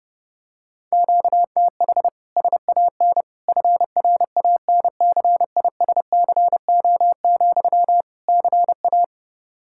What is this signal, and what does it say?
Morse code, heard as a keyed tone, about 32 wpm, 700 Hz. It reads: QTH SAN FRANCISCO, CA
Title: Practice QSO - Exchange 1 - Beginning - 30wpm